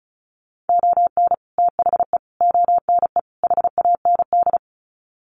Morse code, heard as a keyed tone, about 34 wpm, 700 Hz. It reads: ON THE ONE HAND
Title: Sets of 4 Words 35wpm